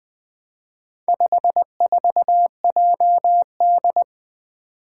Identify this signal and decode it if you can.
Morse code, heard as a keyed tone, about 20 wpm, 700 Hz. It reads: 54JD
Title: Four Challenging Letter-Number Combinations 20wpm